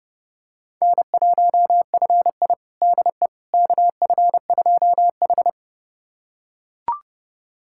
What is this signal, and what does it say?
Morse code, heard as a keyed tone, about 32 wpm, 700 Hz. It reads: N1FI DE KF2H
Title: USA Callsign DE Callsign - 4 Characters - 30wpm